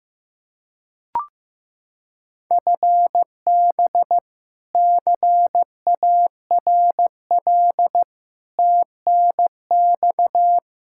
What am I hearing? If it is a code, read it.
Morse code, heard as a keyed tone, about 15 wpm, 700 Hz. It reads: FB CARL TNX
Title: Practice QSO - Exchange 3 - More Info - 15wpm